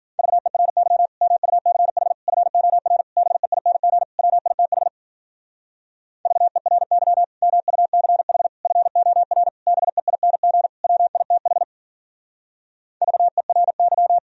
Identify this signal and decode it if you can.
Morse code, heard as a keyed tone, about 48 wpm, 700 Hz. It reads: VERY MUCH FOR BEING WITH VERY MUCH FOR BEING WITH VERY
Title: Speed-Racing - Sets of 5 Words 40wpm